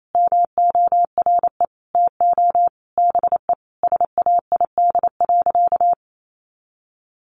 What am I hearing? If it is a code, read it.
Morse code, heard as a keyed tone, about 28 wpm, 700 Hz. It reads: MORE TO BE SAID.
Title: Sentences from Top 500 Words 28wpm